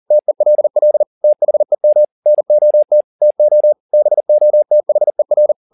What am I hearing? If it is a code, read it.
Morse code, heard as a keyed tone, about 40 wpm, 600 Hz. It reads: TELL THEM NOT TO BOTHER
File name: the old man and the sea 2 40wpm